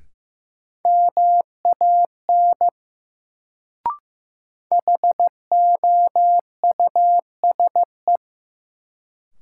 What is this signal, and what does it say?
Morse code, heard as a keyed tone, about 15 wpm, 700 Hz. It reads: MAN HOUSE
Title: Sentences from Top 200 Words - New Word Drill - 15wpm